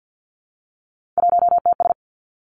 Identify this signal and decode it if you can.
Morse code, heard as a keyed tone, about 50 wpm, 700 Hz. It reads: 1TS
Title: Three Challenging Letter-Number Combinations 50wpm